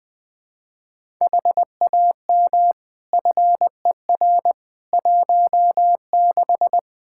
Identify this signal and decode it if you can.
Morse code, heard as a keyed tone, about 21 wpm, 700 Hz. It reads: HAM FER 16
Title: Practice QSO - Exchange 3 - More Info - 20wpm